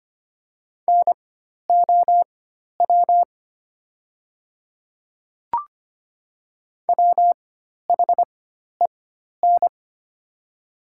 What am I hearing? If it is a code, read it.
Morse code, heard as a keyed tone, about 25 wpm, spaced as for 12 wpm, 700 Hz. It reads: NOW WHEN
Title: 4x Character Spacing - Top 100 Words - 25wpm